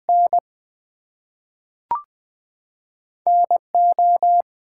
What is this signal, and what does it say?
Morse code, heard as a keyed tone, about 20 wpm, 700 Hz. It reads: N NO
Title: Two Letter Words - 20wpm